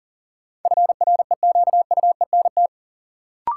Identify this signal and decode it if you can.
Morse code, heard as a keyed tone, about 39 wpm, 700 Hz. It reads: FREQUENT
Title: Sentences from Top 1400 Words - New Word Drill Alt - 40wpm